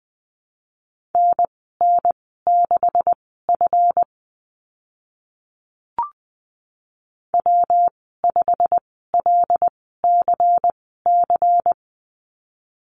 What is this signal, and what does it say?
Morse code, heard as a keyed tone, about 20 wpm, spaced as for 15 wpm, 700 Hz. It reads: NN6F W5LCC
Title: 154 Lesson 25g - Letter p - Call Signs 20wpm